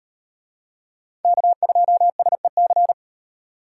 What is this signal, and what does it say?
Morse code, heard as a keyed tone, about 35 wpm, 700 Hz. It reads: K2SEC